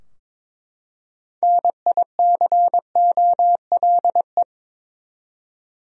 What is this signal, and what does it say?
Morse code, heard as a keyed tone, about 22 wpm, 700 Hz. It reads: NICOLE
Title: USA Common Names - 22wpm